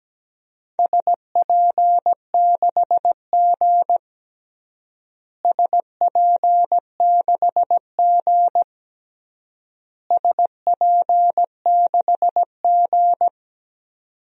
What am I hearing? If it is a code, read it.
Morse code, heard as a keyed tone, about 17 wpm, 700 Hz. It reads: SP6G SP6G SP6G
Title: DXCC-100 Call Signs - 4 Characters - Repeated 3x - 17wpm